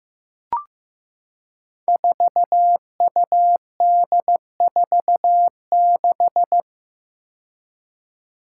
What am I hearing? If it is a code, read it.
Morse code, heard as a keyed tone, about 15 wpm, 700 Hz. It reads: 4UD46
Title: Five Challenging Letter-Number Combinations 15wpm